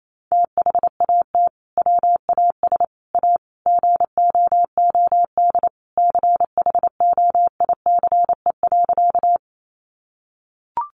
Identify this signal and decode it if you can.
Morse code, heard as a keyed tone, about 28 wpm, 700 Hz. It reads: THAT WAS A GOOD CHOICE.
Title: Sentences from Top 900 Words 28wpm